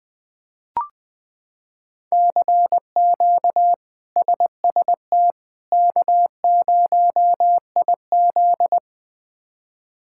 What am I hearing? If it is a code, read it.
Morse code, heard as a keyed tone, about 20 wpm, 700 Hz. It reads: CQ SST K0IZ
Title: K1USN SST QSO Practice - 20wpm